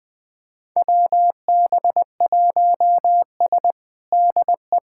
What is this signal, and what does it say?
Morse code, heard as a keyed tone, about 20 wpm, 700 Hz. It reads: WB1S DE
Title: USA Callsign DE Callsign - 4 Characters - 20wpm